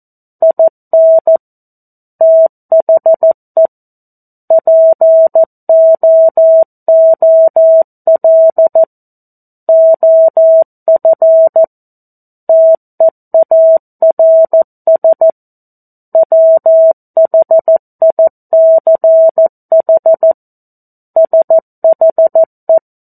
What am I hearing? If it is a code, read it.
Morse code, heard as a keyed tone, about 14 wpm, 650 Hz. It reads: IN THE POOL OF TEARS WHICH SHE
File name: 02-Chapter-02 14 WPM